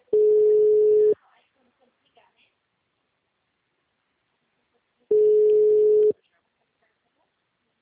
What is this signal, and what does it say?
Ringback tone (Europe, 425 Hz), with speech.